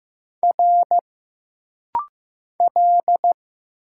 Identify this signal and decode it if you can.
Morse code, heard as a keyed tone, about 15 wpm, 700 Hz. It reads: R L
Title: Single Letters - Rapid-Fire 15wpm